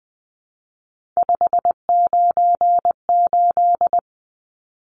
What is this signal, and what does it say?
Morse code, heard as a keyed tone, about 20 wpm, 700 Hz. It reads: 598
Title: Numbers - 3-Digit 20wpm